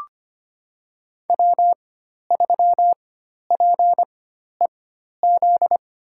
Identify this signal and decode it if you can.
Morse code, heard as a keyed tone, about 25 wpm, spaced as for 12 wpm, 700 Hz. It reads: W3PEZ
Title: USA Call Signs - 4x Character Spacing - 25wpm